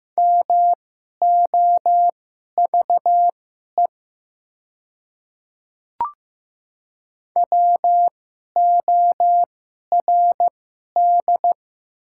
Morse code, heard as a keyed tone, about 15 wpm, spaced as for 11 wpm, 690 Hz. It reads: MOVE WORD
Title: 2x Character Spacing - Top 300 Words - 15wpm